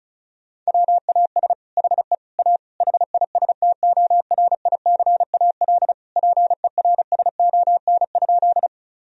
Morse code, heard as a keyed tone, about 36 wpm, 700 Hz. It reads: WAS HE A HISTORICAL PERSON?
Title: Sentences from Top 1000 Words - Review Part 4 - 35wpm